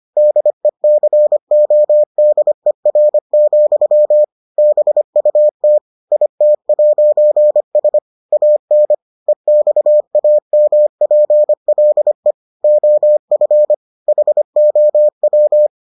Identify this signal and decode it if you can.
Morse code, heard as a keyed tone, about 25 wpm, 600 Hz. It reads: DECODER, BUT IT'S AN EXAMPLE OF HOW